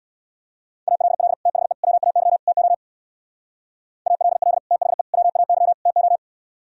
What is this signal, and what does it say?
Morse code, heard as a keyed tone, about 64 wpm, 700 Hz. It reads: ALL THE WAY TO ALL THE WAY TO